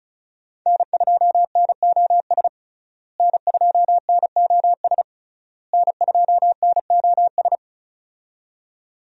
Morse code, heard as a keyed tone, about 36 wpm, 700 Hz. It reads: N2NOS N2NOS N2NOS
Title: USA Call Signs - 5 Characters - Repeated 3x - 35wpm